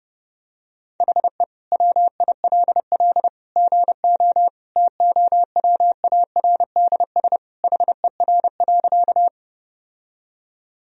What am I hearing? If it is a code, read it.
Morse code, heard as a keyed tone, about 30 wpm, 700 Hz. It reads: HE WILL GO TOWARDS HER.
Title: Sentences from Top 1000 Words - Review Part 2 - 30wpm